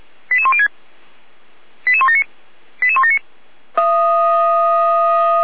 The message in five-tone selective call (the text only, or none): 88106 88179 88179